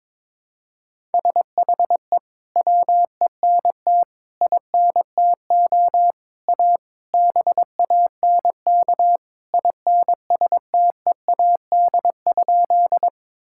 Morse code, heard as a keyed tone, about 22 wpm, 700 Hz. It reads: SHE WENT INTO A BANK INSTEAD?
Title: Sentences from Top 1000 Words - Review Part 6 - 22wpm